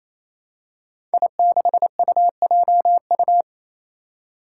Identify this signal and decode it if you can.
Morse code, heard as a keyed tone, about 27 wpm, 700 Hz. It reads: I6UJU
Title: Five Challenging Letter-Number Combinations 28wpm